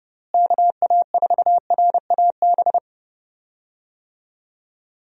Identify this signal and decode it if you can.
Morse code, heard as a keyed tone, about 30 wpm, 700 Hz. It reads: KA4RAB
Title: USA Call Signs - 6 Characters - Repeated 2x - 30wpm